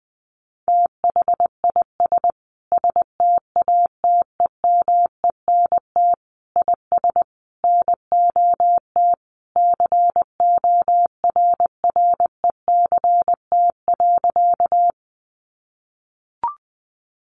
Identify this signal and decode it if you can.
Morse code, heard as a keyed tone, about 20 wpm, 700 Hz. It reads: THIS STATEMENT IS NOT CORRECT.
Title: Sentences from Top 1200 Words 20wpm